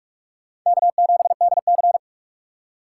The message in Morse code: K7DC

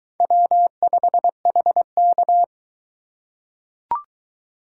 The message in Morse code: W5HK

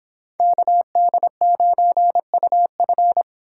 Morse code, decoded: KD9UF

